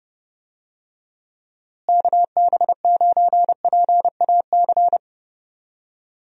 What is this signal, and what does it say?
Morse code, heard as a keyed tone, about 30 wpm, 700 Hz. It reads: KB9PAC
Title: USA Call Signs - 6 Characters - Repeated 2x - 30wpm